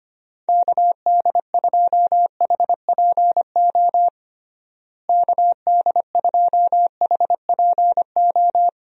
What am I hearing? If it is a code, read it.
Morse code, heard as a keyed tone, about 25 wpm, 700 Hz. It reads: KD2HPO KD2HPO
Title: USA Call Signs - 6 Characters - Repeated 2x - 25wpm